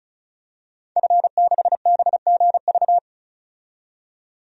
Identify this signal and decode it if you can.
Morse code, heard as a keyed tone, about 36 wpm, 700 Hz. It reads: F6BGV
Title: DXCC-100 Call Signs - 5 Characters - 35wpm